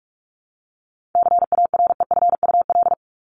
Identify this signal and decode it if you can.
Morse code, heard as a keyed tone, about 44 wpm, 700 Hz. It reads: CAREFUL
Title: Top 1501-1750 Words 45wpm